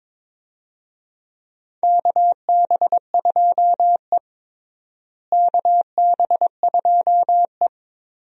Morse code, heard as a keyed tone, about 22 wpm, 700 Hz. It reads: KB2E KB2E
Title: USA Call Signs - 4 Characters - Repeated 3x - 22wpm